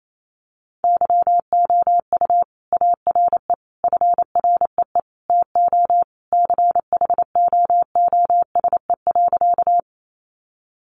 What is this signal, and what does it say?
Morse code, heard as a keyed tone, about 28 wpm, 700 Hz. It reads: YOU ARE FREE TO CHOOSE.
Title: Sentences from Top 700 Words 28wpm